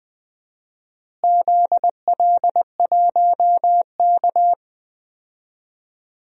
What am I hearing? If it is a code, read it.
Morse code, heard as a keyed tone, about 20 wpm, 700 Hz. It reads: ZL1K
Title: DXCC-100 Call Signs - 4 Characters - Repeated 2x - 20wpm